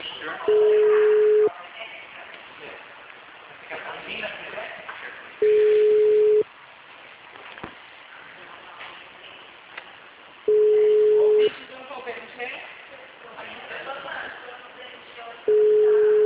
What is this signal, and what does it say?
Ringback tone (Europe, 425 Hz), with speech.